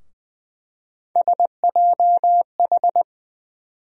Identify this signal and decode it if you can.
Morse code, heard as a keyed tone, about 20 wpm, 700 Hz. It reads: SJH
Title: USA Call Sign Postfixes - 3 Letters 20wpm